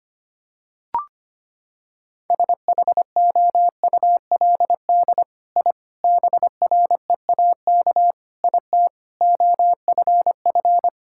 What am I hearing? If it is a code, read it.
Morse code, heard as a keyed tone, about 25 wpm, 700 Hz. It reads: SHOULD I BREAK IT OFF
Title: Sentences from Top 1000 Words - Review Part 6 - 25wpm